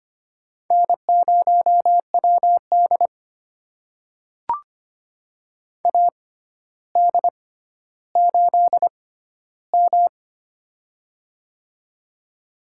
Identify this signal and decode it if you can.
Morse code, heard as a keyed tone, about 25 wpm, spaced as for 13 wpm, 700 Hz. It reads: N0WD AD8M
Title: USA Call Signs - 4 Characters - 6x Character Spacing - 25wpm